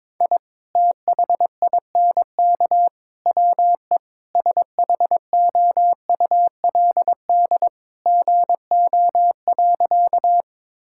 Morse code, heard as a keyed tone, about 22 wpm, 700 Hz. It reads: I THINK WE SHOULD GO.
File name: practice / 22wpm / Sentences with 3-Word Phrases - 22wpm